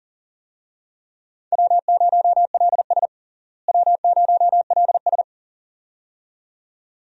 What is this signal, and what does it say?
Morse code, heard as a keyed tone, about 37 wpm, 700 Hz. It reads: W0LS W0LS
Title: USA Call Signs - 4 Characters - Repeated 2x - 40wpm